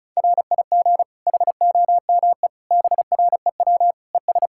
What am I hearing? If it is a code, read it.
Morse code, heard as a keyed tone, about 36 wpm, 700 Hz. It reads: RIG HOME BREW ES
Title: Practice QSO - Exchange 2 - Info - 35wpm